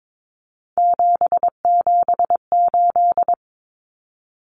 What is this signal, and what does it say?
Morse code, heard as a keyed tone, about 22 wpm, 700 Hz. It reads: 778
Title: Numbers - 3-Digit 22wpm